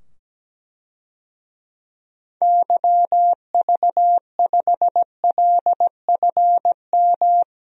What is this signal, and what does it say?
Morse code, heard as a keyed tone, about 17 wpm, 700 Hz. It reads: YV5LFM
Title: DXCC-100 Call Signs - 6 Characters - 17wpm